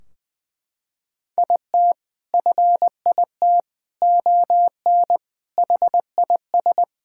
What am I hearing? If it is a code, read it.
Morse code, heard as a keyed tone, about 20 wpm, 700 Hz. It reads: IT FIT ON HIS
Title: Sentences from Top 1600 Words 20wpm